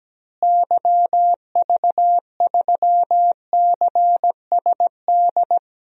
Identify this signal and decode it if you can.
Morse code, heard as a keyed tone, about 17 wpm, 700 Hz. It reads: YV3CSD
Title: DXCC-100 Call Signs - 6 Characters - Repeated 2x - 17wpm